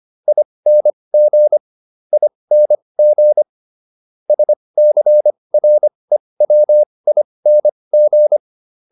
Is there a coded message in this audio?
Yes: Morse code from 0.3 to 8.4 s, heard as a keyed tone, about 25 wpm, spaced as for 20 wpm, 600 Hz. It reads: ING ING SCREWING